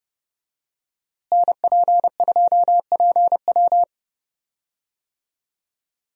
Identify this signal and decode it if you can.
Morse code, heard as a keyed tone, about 31 wpm, 700 Hz. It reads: NP2PW
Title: USA Call Signs - 5 Characters - Repeated 3x - 30wpm